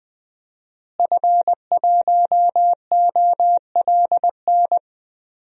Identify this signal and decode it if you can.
Morse code, heard as a keyed tone, about 21 wpm, 700 Hz. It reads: F1OLN
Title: DXCC-100 Call Signs - 5 Characters - Repeated 3x - 20wpm